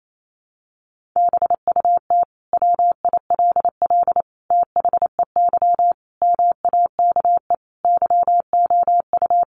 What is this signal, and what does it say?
Morse code, heard as a keyed tone, about 28 wpm, 700 Hz. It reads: BUT WILL THEY MAKE YOU
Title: Sentences from Top 1000 Words 28wpm